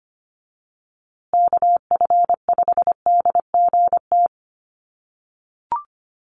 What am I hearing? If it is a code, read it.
Morse code, heard as a keyed tone, about 25 wpm, 700 Hz. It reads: KF5DGT
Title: USA Call Signs Repeated 3x 25wpm